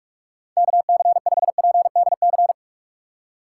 Morse code, heard as a keyed tone, about 46 wpm, 700 Hz. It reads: KC5PDC
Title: USA Call Signs - 2x Character Spacing - 45wpm